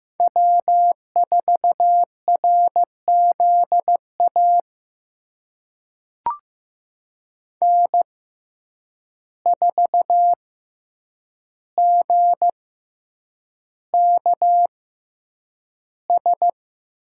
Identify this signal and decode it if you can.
Morse code, heard as a keyed tone, about 15 wpm, spaced as for 8 wpm, 700 Hz. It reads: W4RZA N4GKS
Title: USA Call Signs - 5 Characters - 6x Character Spacing - 15wpm